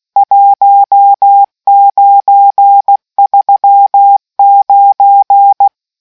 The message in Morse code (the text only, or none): 1939